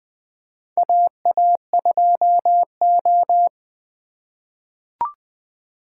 Morse code, heard as a keyed tone, about 21 wpm, 700 Hz. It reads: AA2O
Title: USA Call Signs - 3x Character Spacing - 20wpm